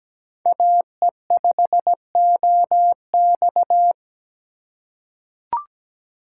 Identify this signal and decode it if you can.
Morse code, heard as a keyed tone, about 17 wpm, 700 Hz. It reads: AE5OX
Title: USA Call Signs - 5 Characters - 3x Character Spacing - 17wpm